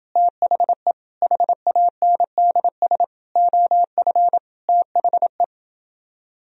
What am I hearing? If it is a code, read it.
Morse code, heard as a keyed tone, about 29 wpm, 700 Hz. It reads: THE HANDS OF THE